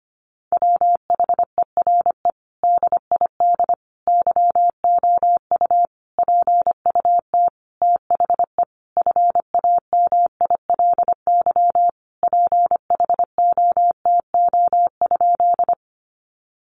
Morse code, heard as a keyed tone, about 25 wpm, 700 Hz. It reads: WHERE DID YOU PUT THE FAMILY PHOTO?
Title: Sentences from Top 1200 Words 25wpm